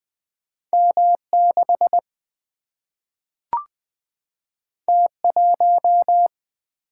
Morse code, heard as a keyed tone, about 20 wpm, 700 Hz. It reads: M6 T1
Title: Single-Letter Single-Number 20wpm